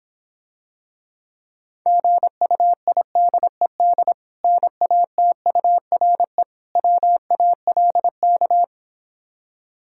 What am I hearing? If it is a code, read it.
Morse code, heard as a keyed tone, about 26 wpm, 700 Hz. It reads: GUIDED NATURE WALK